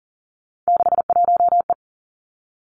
Morse code, heard as a keyed tone, about 41 wpm, 700 Hz. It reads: 61E
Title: Three Challenging Letter-Number Combinations 40wpm